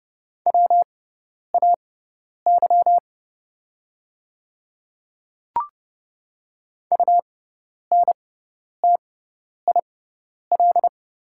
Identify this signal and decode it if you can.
Morse code, heard as a keyed tone, about 30 wpm, spaced as for 10 wpm, 690 Hz. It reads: WAY UNTIL
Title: 6x Character Spacing - Top 300 Words - 30wpm